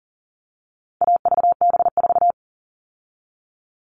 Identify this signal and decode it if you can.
Morse code, heard as a keyed tone, about 40 wpm, 700 Hz. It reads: AVB4
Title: All Cut Numbers - 4-Digit 40wpm